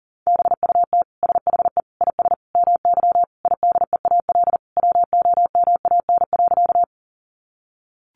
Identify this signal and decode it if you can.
Morse code, heard as a keyed tone, about 37 wpm, 700 Hz. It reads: BUT SHE IS MY IDEAL WOMAN.
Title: Sentences from Top 1400 Words 40wpm